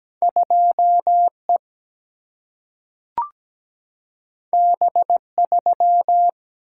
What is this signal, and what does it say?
Morse code, heard as a keyed tone, about 17 wpm, 700 Hz. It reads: 2E B3
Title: Single-Letter Single-Number 17wpm